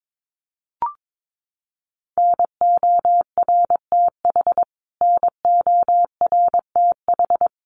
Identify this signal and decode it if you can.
Morse code, heard as a keyed tone, about 22 wpm, 700 Hz. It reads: NORTH NORTH
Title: Top 500 Words Repeated 22wpm